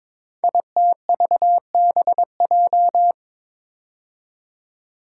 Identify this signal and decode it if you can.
Morse code, heard as a keyed tone, about 22 wpm, 700 Hz. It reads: ITVBJ
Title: Code Groups - 5 Letters - 22wpm